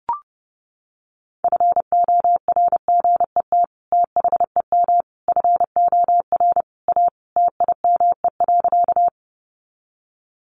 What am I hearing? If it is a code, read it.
Morse code, heard as a keyed tone, about 30 wpm, 700 Hz. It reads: FORGET THEM FOR A TIME.
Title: Sentences from Top 1200 Words 30wpm